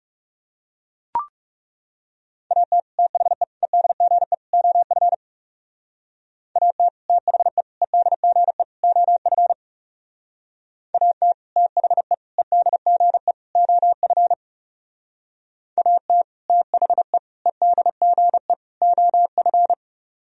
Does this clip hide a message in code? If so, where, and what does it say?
Yes: Morse code from 2.5 to 19.7 s, heard as a keyed tone, about 38 wpm, 700 Hz. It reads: AT THE EDGE OF AT THE EDGE OF AT THE EDGE OF AT THE EDGE OF